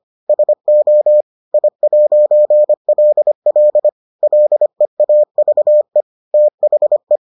Morse code, heard as a keyed tone, about 25 wpm, 600 Hz. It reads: SO I'LL LEAVE THE